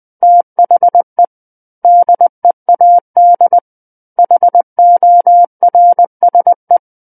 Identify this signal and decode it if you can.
Morse code, heard as a keyed tone, about 20 wpm, 700 Hz. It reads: THE DEAD HORSE
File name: 20 WPM